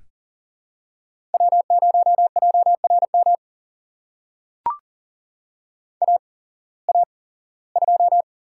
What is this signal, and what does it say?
Morse code, heard as a keyed tone, about 40 wpm, spaced as for 21 wpm, 700 Hz. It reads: W0JRM AA2
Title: USA Call Signs - 5 Characters - 8x Character Spacing - 40wpm